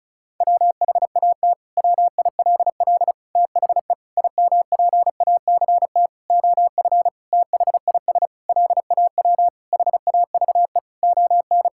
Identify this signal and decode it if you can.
Morse code, heard as a keyed tone, about 36 wpm, 700 Hz. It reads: WHAT WILL THE IMPACT OF THIS LAW HAVE ON
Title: Sentences from Top 1000 Words - Review Part 6 - 35wpm